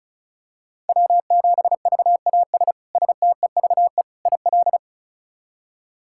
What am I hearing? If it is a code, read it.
Morse code, heard as a keyed tone, about 36 wpm, 700 Hz. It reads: W7VAS STEVE IL
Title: K1USN SST QSO Practice - 35wpm